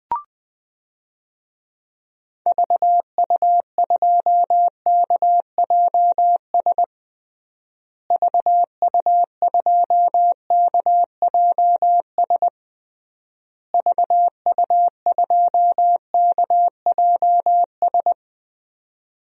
Morse code, heard as a keyed tone, about 20 wpm, 700 Hz. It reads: VU2KJS VU2KJS VU2KJS
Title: DXCC-100 Call Signs - 6 Characters - Repeated 3x - 20wpm